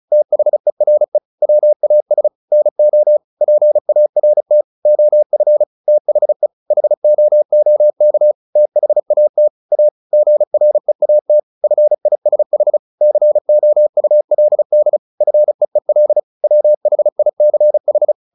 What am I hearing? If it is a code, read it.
Morse code, heard as a keyed tone, about 34 wpm, 600 Hz. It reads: THERE WAS NO PART OF THE HOOK THAT A GREAT FISH COULD FEEL WHICH
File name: the old man and the sea / the old man and the sea 1 35wpm